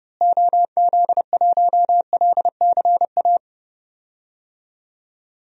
Morse code, heard as a keyed tone, about 30 wpm, 700 Hz. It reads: OZ1LCA
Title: DXCC-100 Call Signs - 6 Characters - Repeated 2x - 30wpm